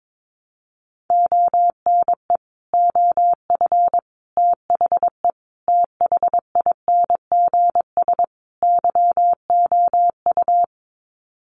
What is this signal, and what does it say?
Morse code, heard as a keyed tone, about 22 wpm, 700 Hz. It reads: ONE OF THE THINGS YOU
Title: Sets of 5 Words Encore2 22wpm